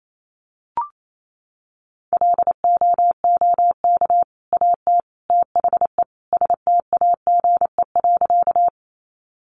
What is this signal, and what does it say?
Morse code, heard as a keyed tone, about 28 wpm, 700 Hz. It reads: LOOK AT THE STAGE.
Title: Sentences from Top 1200 Words 28wpm